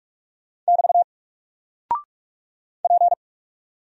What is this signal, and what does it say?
Morse code, heard as a keyed tone, about 46 wpm, 700 Hz. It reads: <BT> P
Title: Single Character (Letter:Number:Punctuation:Prosign) Alt 45wpm